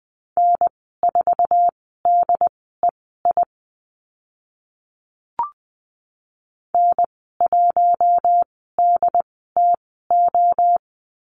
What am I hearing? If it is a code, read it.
Morse code, heard as a keyed tone, about 20 wpm, spaced as for 15 wpm, 700 Hz. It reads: N4DEI N1DTO
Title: 076 Lesson 13g - Letter L - Call Signs 20wpm